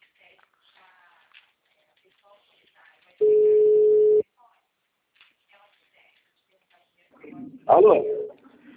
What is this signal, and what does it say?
Ringback tone (Europe, 425 Hz), with speech.